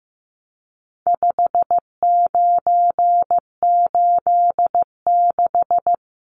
Morse code, heard as a keyed tone, about 15 wpm, 700 Hz. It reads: 5986
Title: Numbers - 4-Digit 15wpm